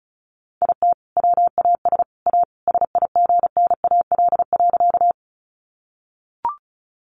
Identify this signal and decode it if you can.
Morse code, heard as a keyed tone, about 34 wpm, 700 Hz. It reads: IT WAS A SIGNAL.
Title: Sentences from Top 1400 Words 35wpm